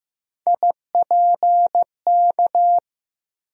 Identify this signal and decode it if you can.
Morse code, heard as a keyed tone, about 15 wpm, 700 Hz. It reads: IPK
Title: USA Call Sign Postfixes - 3 Letters 15wpm